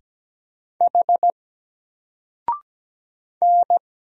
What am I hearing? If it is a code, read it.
Morse code, heard as a keyed tone, about 17 wpm, 700 Hz. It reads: H N
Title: Single Letter-Number 17wpm